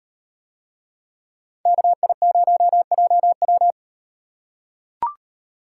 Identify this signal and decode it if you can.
Morse code, heard as a keyed tone, about 38 wpm, 700 Hz. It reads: KI0JW